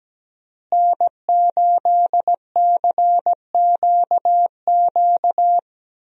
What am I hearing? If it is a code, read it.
Morse code, heard as a keyed tone, about 17 wpm, 700 Hz. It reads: N8CQQ
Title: USA Call Signs - 5 Characters - Repeated 3x - 17wpm